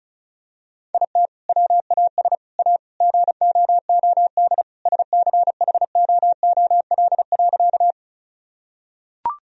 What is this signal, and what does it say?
Morse code, heard as a keyed tone, about 36 wpm, 700 Hz. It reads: IT WAS A GOOD SCHOOL.
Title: Sentences from Top 1000 Words - Review Part 1 - 35wpm